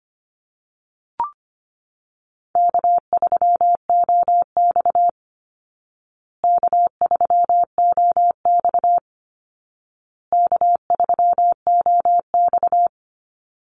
Morse code, heard as a keyed tone, about 25 wpm, 700 Hz. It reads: K3OX K3OX K3OX
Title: Call Signs Repeated 2x - 25wpm